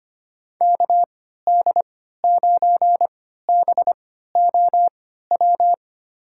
Morse code, heard as a keyed tone, about 25 wpm, spaced as for 14 wpm, 700 Hz. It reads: KD9BOW
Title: USA Call Signs - 6 Characters - 3x Character Spacing - 25wpm